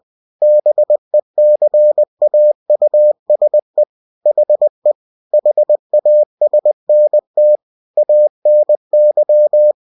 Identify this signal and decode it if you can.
Morse code, heard as a keyed tone, about 20 wpm, 600 Hz. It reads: BECAUSE HE HASNT ANY
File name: cw-books / winnie-the-pooh / winnie-20wpm-page-0011